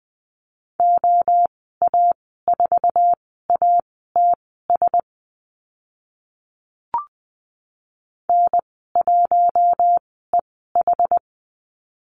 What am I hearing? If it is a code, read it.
Morse code, heard as a keyed tone, about 20 wpm, spaced as for 15 wpm, 700 Hz. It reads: OA4ATS N1EH
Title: 082 Lesson 14g - Number 2 - Call Signs 20wpm